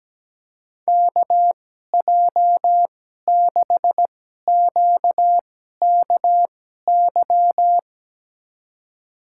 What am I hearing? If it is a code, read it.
Morse code, heard as a keyed tone, about 17 wpm, spaced as for 12 wpm, 700 Hz. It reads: KJ6QKY
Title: USA Call Signs - 6 Characters - 2x Character Spacing - 17wpm